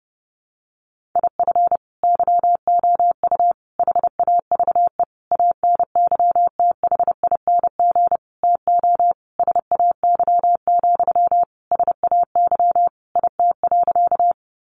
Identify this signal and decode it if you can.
Morse code, heard as a keyed tone, about 28 wpm, 700 Hz. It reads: IF YOU HAVE ANYTHING TO SAY, SAY IT.
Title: Sentences from Top 1000 Words 30wpm